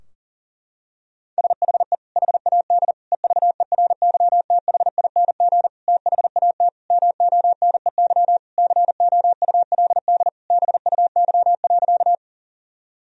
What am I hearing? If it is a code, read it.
Morse code, heard as a keyed tone, about 40 wpm, 700 Hz. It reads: SHE HAD EVERYTHING THAT MONEY COULD BUY.
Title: Sentences from Top 1000 Words - Review Part 4 - 40wpm